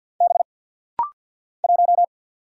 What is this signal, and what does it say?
Morse code, heard as a keyed tone, about 48 wpm, 690 Hz. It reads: B 1
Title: Single Letter-Number - Rapid-Fire 50wpm